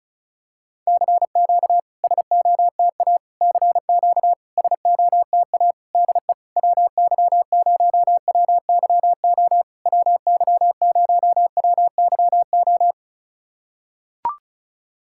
Morse code, heard as a keyed tone, about 36 wpm, 690 Hz. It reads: CQ SOTA CQ SOTA DE WY0WYO WY0WYO
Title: SOTA QSO Practice 35wpm